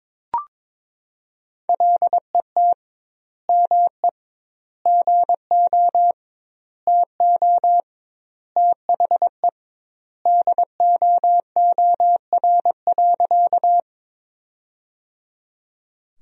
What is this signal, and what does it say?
Morse code, heard as a keyed tone, about 22 wpm, 700 Hz. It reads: LET ME GO TO THE DOOR.
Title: 2x Word-Spacing - Sentences from Top 300 Words - 22wpm